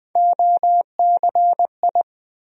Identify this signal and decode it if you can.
Morse code, heard as a keyed tone, about 20 wpm, 700 Hz. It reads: OCI
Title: USA Call Sign Postfixes - 3 Letters 20wpm